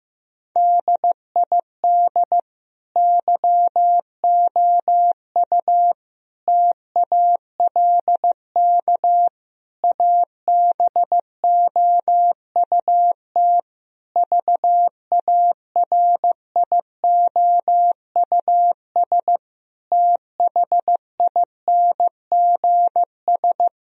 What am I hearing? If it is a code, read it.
Morse code, heard as a keyed tone, about 15 wpm, 700 Hz. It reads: DID YOU TALK ABOUT VARIOUS THINGS
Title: Sentences from Top 1000 Words - Review Part 4 - 15wpm